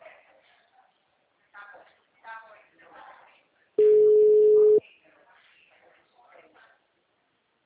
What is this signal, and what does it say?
Ringback tone (Europe, 425 Hz), with speech.